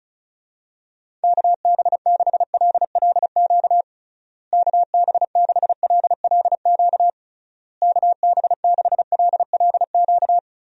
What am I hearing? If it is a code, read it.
Morse code, heard as a keyed tone, about 34 wpm, 700 Hz. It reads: KB6LLQ KB6LLQ KB6LLQ
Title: USA Call Signs - 6 Characters - Repeated 3x - 35wpm